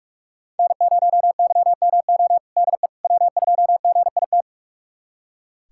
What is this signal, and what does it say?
Morse code, heard as a keyed tone, about 43 wpm, 700 Hz. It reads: N0YMO DE W2GIT
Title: USA Callsign DE Callsign - 5 Characters - 45wpm